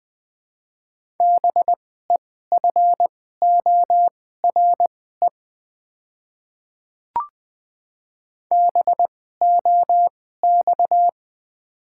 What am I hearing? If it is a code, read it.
Morse code, heard as a keyed tone, about 20 wpm, spaced as for 14 wpm, 700 Hz. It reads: BEFORE BOX
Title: 2x Character Spacing - Top 500 Words - 20wpm